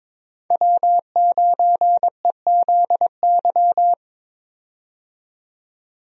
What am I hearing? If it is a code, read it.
Morse code, heard as a keyed tone, about 22 wpm, 700 Hz. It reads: W9EZY
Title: USA Call Signs - 5 Characters - 22wpm